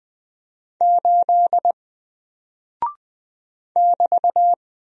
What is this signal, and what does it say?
Morse code, heard as a keyed tone, about 20 wpm, 700 Hz. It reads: 8 <BT>
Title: Single Character (Letter:Number:Punctuation:Prosign) Alt 20wpm